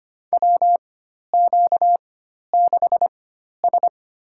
Morse code, heard as a keyed tone, about 25 wpm, spaced as for 12 wpm, 700 Hz. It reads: WQ6S
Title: USA Call Signs - 4 Characters - 4x Character Spacing - 25wpm